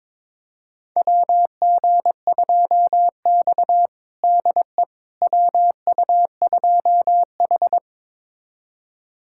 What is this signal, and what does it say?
Morse code, heard as a keyed tone, about 21 wpm, 700 Hz. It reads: WG2X DE WU2H
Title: USA Callsign DE Callsign - 4 Characters - 22wpm